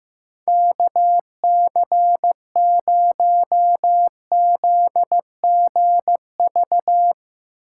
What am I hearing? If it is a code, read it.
Morse code, heard as a keyed tone, about 15 wpm, 700 Hz. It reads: KC0ZGV
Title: Speed Racing - USA 2x3 Call Signs 15wpm